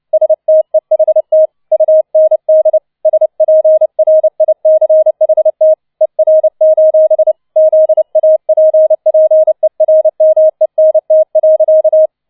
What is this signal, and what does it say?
Morse code, heard as a keyed tone, about 29 wpm, 630 Hz. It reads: STEHT UND SPRICHT ER: ZAPPERMENT.